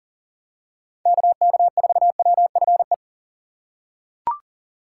Morse code, heard as a keyed tone, about 40 wpm, 700 Hz. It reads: KK4WFE
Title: USA Call Signs - 6 Characters - 3x Character Spacing - 40wpm